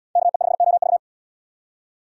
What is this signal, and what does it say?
Morse code, heard as a keyed tone, about 78 wpm, 700 Hz. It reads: KE5OV